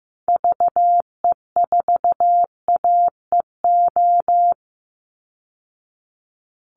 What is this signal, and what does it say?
Morse code, heard as a keyed tone, about 14 wpm, 700 Hz. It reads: VE4AEO
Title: Call Signs Repeated 3x - 15wpm